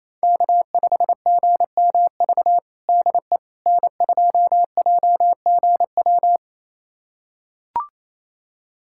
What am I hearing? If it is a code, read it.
Morse code, heard as a keyed tone, about 28 wpm, 700 Hz. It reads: K5GMV DE N2JGW
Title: USA Callsign DE Callsign - 5 Characters - 28wpm